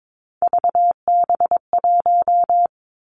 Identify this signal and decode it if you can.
Morse code, heard as a keyed tone, about 22 wpm, 700 Hz. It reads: VB1
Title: Three Challenging Letter-Number Combinations 22wpm